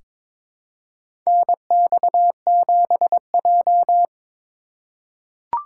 Morse code, heard as a keyed tone, about 22 wpm, 700 Hz. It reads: NX7J
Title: USA Call Signs - 4 Characters - 2x Character Spacing - 22wpm